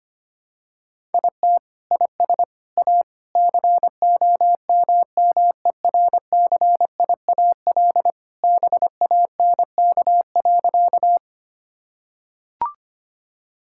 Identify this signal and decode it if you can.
Morse code, heard as a keyed tone, about 25 wpm, 700 Hz. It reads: IT IS A COMMERCIAL BANK.
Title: Sentences from Top 1000 Words - Review Part 6 - 25wpm